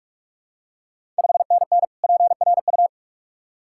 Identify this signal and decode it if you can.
Morse code, heard as a keyed tone, about 46 wpm, 700 Hz. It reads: 5NN PRU
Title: Ontario QSO Party Exchange 45wpm